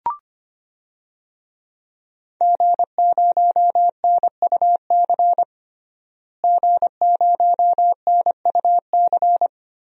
Morse code, heard as a keyed tone, about 25 wpm, 700 Hz. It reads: G0NUC G0NUC
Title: DXCC-100 Call Signs - 5 Characters - Repeated 2x - 25wpm